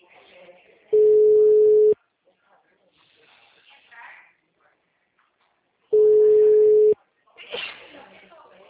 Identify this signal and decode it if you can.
Ringback tone (Europe, 425 Hz), with speech.